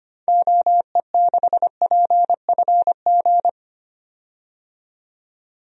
Morse code, heard as a keyed tone, about 25 wpm, 700 Hz. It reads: OE6PFG
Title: DXCC-100 Call Signs - 6 Characters - 25wpm